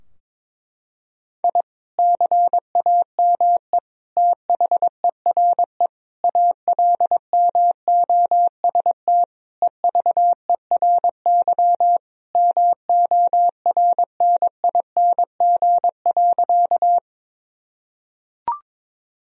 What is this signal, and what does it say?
Morse code, heard as a keyed tone, about 22 wpm, 700 Hz. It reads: I CAME THERE ALMOST EVERY MORNING.
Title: Sentences from Top 1000 Words - Review Part 4 - 22wpm